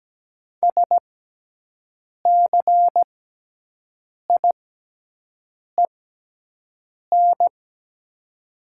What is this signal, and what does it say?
Morse code, heard as a keyed tone, about 17 wpm, spaced as for 6 wpm, 700 Hz. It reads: SCIEN
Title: 6x Character Spacing - Top 500 Words - 17wpm